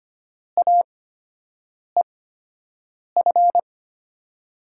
Morse code, heard as a keyed tone, about 25 wpm, spaced as for 7 wpm, 700 Hz. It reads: AEF
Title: USA Call Signs - 6 Characters - 8x Character Spacing - 25wpm